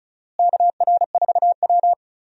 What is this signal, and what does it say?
Morse code, heard as a keyed tone, about 34 wpm, 700 Hz. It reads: KR4W
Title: USA Call Signs - 4 Characters - Repeated 3x - 35wpm